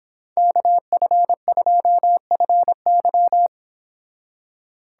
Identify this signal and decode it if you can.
Morse code, heard as a keyed tone, about 25 wpm, 700 Hz. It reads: KF2FY